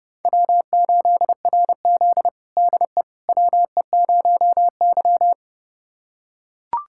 Morse code, heard as a keyed tone, about 30 wpm, 700 Hz. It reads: W8RZ DE WE0Y
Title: USA Callsign DE Callsign - 4 Characters - 30wpm